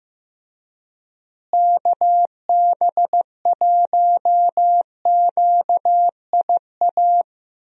Morse code, heard as a keyed tone, about 16 wpm, 700 Hz. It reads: KB1QIA
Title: USA Call Signs - 6 Characters - Repeated 3x - 15wpm